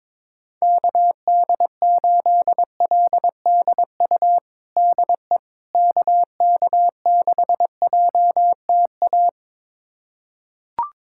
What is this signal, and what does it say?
Morse code, heard as a keyed tone, about 22 wpm, 700 Hz. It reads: KD8LDU DE KK6JTA
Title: USA Callsign DE Callsign - 6 Characters - 22wpm